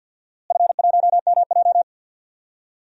Morse code, heard as a keyed tone, about 47 wpm, 700 Hz. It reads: F1MJ
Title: DXCC-100 Call Signs - 4 Characters - Repeated 2x - 50wpm